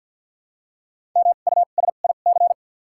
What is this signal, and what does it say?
Morse code, heard as a keyed tone, about 53 wpm, spaced as for 37 wpm, 700 Hz. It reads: MUSIC
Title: 2x Character Spacing - Top 300 Words - 50wpm